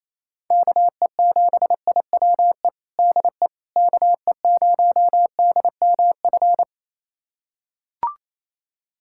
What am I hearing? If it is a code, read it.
Morse code, heard as a keyed tone, about 28 wpm, 700 Hz. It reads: KE7IWE DE KE0DMF
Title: USA Callsign DE Callsign 28wpm